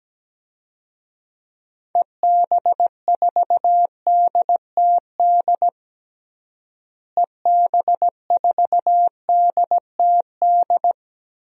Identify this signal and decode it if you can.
Morse code, heard as a keyed tone, about 17 wpm, 700 Hz. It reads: EB4DTD EB4DTD
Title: DXCC-100 Call Signs - 6 Characters - Repeated 3x - 17wpm